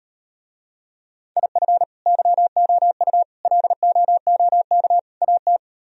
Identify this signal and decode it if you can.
Morse code, heard as a keyed tone, about 38 wpm, 700 Hz. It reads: IF YOU LOOK AT